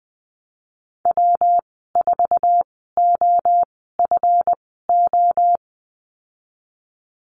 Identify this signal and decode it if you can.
Morse code, heard as a keyed tone, about 19 wpm, spaced as for 14 wpm, 700 Hz. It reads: W4OFO
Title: 140 Lesson 23g - Letter f - Call Signs 20wpm